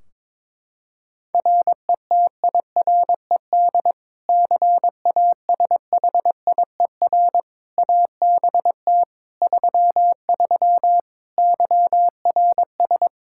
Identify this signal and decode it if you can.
Morse code, heard as a keyed tone, about 21 wpm, 700 Hz. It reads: RETIRED CASHIER ABT 33 YRS
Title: Practice QSO - Exchange 3 - More Info - 22wpm